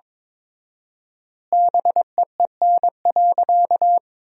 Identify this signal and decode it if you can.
Morse code, heard as a keyed tone, about 22 wpm, 700 Hz. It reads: BEEN.
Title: 4x Word-Spacing - Sentences from Top 300 Words - 22wpm